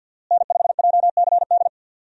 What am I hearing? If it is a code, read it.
Morse code, heard as a keyed tone, about 51 wpm, 700 Hz. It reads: N5JCD
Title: USA Call Signs - 5 Characters - Repeated 2x - 50wpm